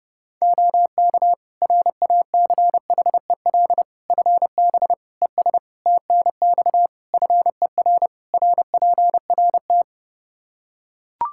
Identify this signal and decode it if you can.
Morse code, heard as a keyed tone, about 30 wpm, 700 Hz. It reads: OK RACHEL FB ES TNX FER RPRT
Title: Practice QSO - Exchange 2 - Info - 30wpm